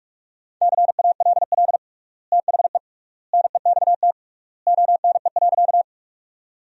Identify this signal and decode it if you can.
Morse code, heard as a keyed tone, about 46 wpm, 700 Hz. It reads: CALL THE NEXT ONE.
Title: 3x Word-Spacing - Sentences from Top 300 Words - 45wpm